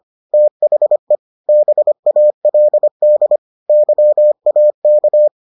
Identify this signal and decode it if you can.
Morse code, heard as a keyed tone, about 25 wpm, 600 Hz. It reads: THE BALD YAK